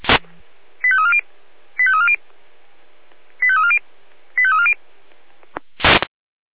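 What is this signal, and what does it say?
Five-tone selective call (ZVEI), heard as quick runs of five tones, tones of about 70 ms. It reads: 85340 85340 85340 85340
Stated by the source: ZVEI five-tone call received by SDR.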